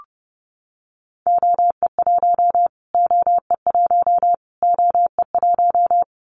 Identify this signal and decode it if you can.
Morse code, heard as a keyed tone, about 30 wpm, 700 Hz. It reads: OE1 OE1 OE1
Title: Callsign Prefixes for Top 25 DXCC - Repeated 3x - 30wpm